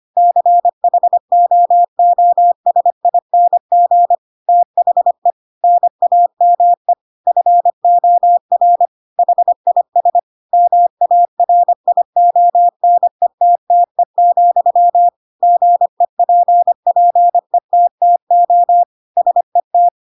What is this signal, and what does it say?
Morse code, heard as a keyed tone, about 25 wpm, 700 Hz. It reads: CHOOSING THE NAME FOR HIS MARIONETTE, GEPPETTO SET
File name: Pinocchio / Pinocchio ch3 25 25 ik1hge